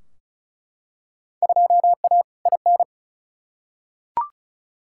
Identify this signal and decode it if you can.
Morse code, heard as a keyed tone, about 36 wpm, 700 Hz. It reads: 2A IN
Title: Field Day Exchanges - 35wpm